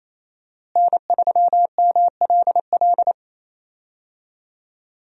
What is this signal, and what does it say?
Morse code, heard as a keyed tone, about 28 wpm, 700 Hz. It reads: N3MLL
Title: USA Call Signs - 5 Characters - Repeated 3x - 28wpm